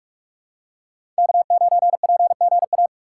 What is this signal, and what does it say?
Morse code, heard as a keyed tone, about 41 wpm, 700 Hz. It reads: K9PGA
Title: USA Call Signs - 5 Characters - 6x Character Spacing - 45wpm